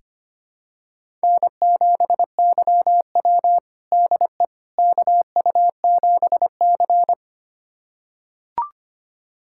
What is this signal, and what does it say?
Morse code, heard as a keyed tone, about 25 wpm, 700 Hz. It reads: N7YW DE KU7C
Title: USA Callsign DE Callsign - 4 Characters - 25wpm